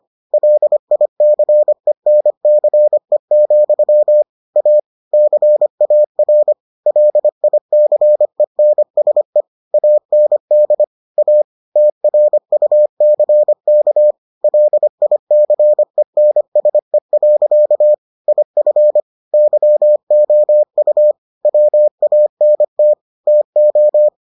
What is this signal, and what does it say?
Morse code, heard as a keyed tone, about 25 wpm, 600 Hz. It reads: LICENCE, A CAR LICENSE AND A TRUCK LICENSE. IF YOU WANT TO